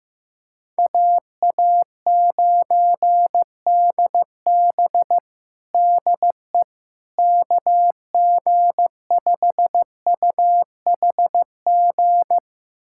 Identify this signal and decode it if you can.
Morse code, heard as a keyed tone, about 15 wpm, 700 Hz. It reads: AA9DB DE KG5UHG
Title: USA Callsign DE Callsign 15wpm